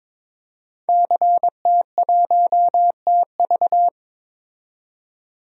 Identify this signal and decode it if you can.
Morse code, heard as a keyed tone, about 22 wpm, 700 Hz. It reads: CT1TV
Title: DXCC-100 Call Signs - 5 Characters - 22wpm